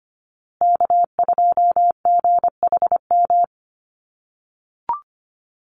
Morse code, heard as a keyed tone, about 25 wpm, 700 Hz. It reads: K2GHM
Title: USA Call Signs Repeated 3x 25wpm